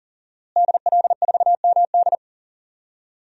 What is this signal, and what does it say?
Morse code, heard as a keyed tone, about 40 wpm, 700 Hz. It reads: DL4MD
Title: DXCC-100 Call Signs - 5 Characters - Repeated 3x - 40wpm